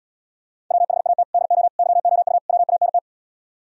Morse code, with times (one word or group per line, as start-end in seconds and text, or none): 0.7-1.2 WHAT
1.3-1.7 DO
1.8-2.4 YOU
2.5-3.0 WANT